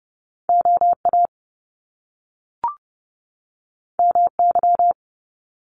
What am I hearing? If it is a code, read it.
Morse code, heard as a keyed tone, about 30 wpm, 700 Hz. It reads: OA MY
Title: Top Two Letter Combinations in Words 30wpm